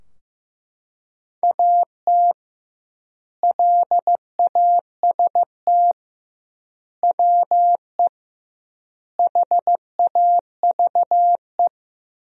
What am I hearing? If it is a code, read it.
Morse code, heard as a keyed tone, about 15 wpm, 700 Hz. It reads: AT LAST WE HAVE
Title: 2x Word-Spacing - Sentences from Top 300 Words - 15wpm